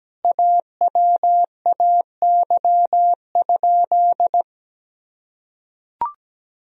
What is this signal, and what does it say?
Morse code, heard as a keyed tone, about 17 wpm, 700 Hz. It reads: AWAY?
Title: Sentences from Top 1000 Words - Review Part 1 - 17wpm